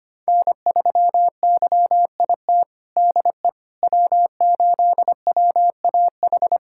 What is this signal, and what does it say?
Morse code, heard as a keyed tone, about 25 wpm, 700 Hz. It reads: N3YIT DE W8WAH
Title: USA Callsign DE Callsign - 5 Characters - 25wpm